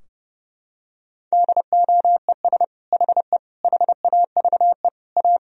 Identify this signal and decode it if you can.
Morse code, heard as a keyed tone, about 30 wpm, 700 Hz. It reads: DOES HE HAVE A
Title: Sentences from Top 1000 Words - Review Part 4 - 30wpm